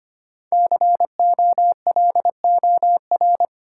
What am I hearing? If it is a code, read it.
Morse code, heard as a keyed tone, about 25 wpm, 700 Hz. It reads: COLOR
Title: Sentences from Top 1000 Words - Review Part 2 - New Word Drill - 25wpm